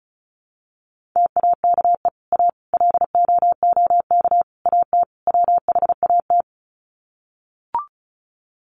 Morse code, heard as a keyed tone, about 34 wpm, 700 Hz. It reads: TAKE A LOOK AT WHAT
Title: Sets of 5 Words Encore2 35wpm